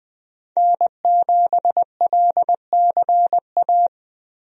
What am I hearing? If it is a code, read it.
Morse code, heard as a keyed tone, about 21 wpm, 700 Hz. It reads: N7LCA
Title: USA Call Signs - 5 Characters - Repeated 2x - 20wpm